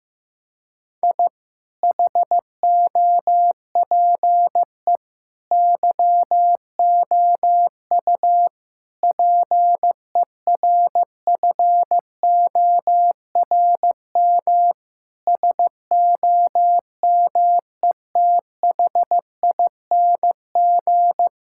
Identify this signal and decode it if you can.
Morse code, heard as a keyed tone, about 15 wpm, 700 Hz. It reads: I HOPE YOU PERFORM SOMETHING
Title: Sentences from Top 1000 Words - Review Part 6 - 15wpm